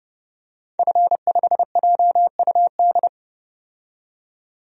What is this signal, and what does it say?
Morse code, heard as a keyed tone, about 30 wpm, 700 Hz. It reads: F5JUD
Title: DXCC-100 Call Signs - 5 Characters - 30wpm